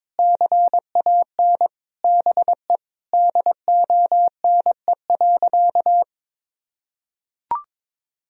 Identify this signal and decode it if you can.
Morse code, heard as a keyed tone, about 22 wpm, 700 Hz. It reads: CAN BE DONE.
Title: Sentences from Top 1000 Words - Review Part 1 - 22wpm